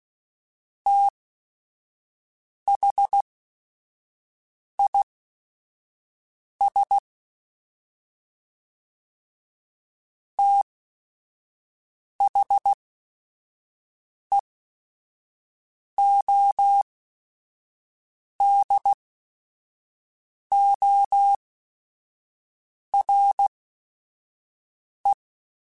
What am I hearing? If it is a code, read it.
Morse code, heard as a keyed tone, about 16 wpm, spaced as for 5 wpm, 800 Hz. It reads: THIS THEODORE
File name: TH Runs / TH Runs mp3s / TH Runs @5